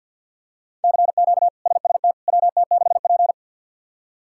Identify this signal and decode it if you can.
Morse code, heard as a keyed tone, about 48 wpm, 700 Hz. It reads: CQ SST WT6P
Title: K1USN SST QSO Practice - 50wpm